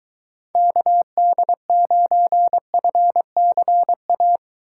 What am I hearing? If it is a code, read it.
Morse code, heard as a keyed tone, about 23 wpm, 700 Hz. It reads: KD9FCA